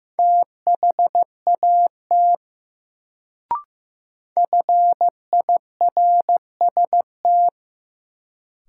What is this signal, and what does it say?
Morse code, heard as a keyed tone, about 15 wpm, 700 Hz. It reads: THAT FIRST
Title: Sentences from Top 100 Words - New Word Drill - 15wpm